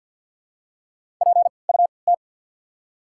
Morse code, heard as a keyed tone, about 54 wpm, 690 Hz. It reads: P U T
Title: 3x Character Spacing - Top 200 Words - 50wpm